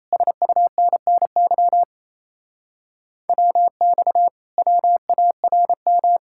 Morse code, heard as a keyed tone, about 36 wpm, 700 Hz. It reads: SUNNY WX WARM